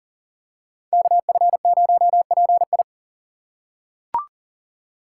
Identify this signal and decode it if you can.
Morse code, heard as a keyed tone, about 40 wpm, 700 Hz. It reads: KF0PI
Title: USA Call Signs - 5 Characters - 2x Character Spacing - 40wpm